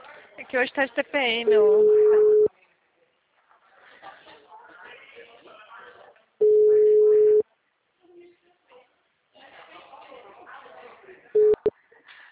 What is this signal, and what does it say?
Ringback tone (Europe, 425 Hz), with speech.
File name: out- -2005- - - .15150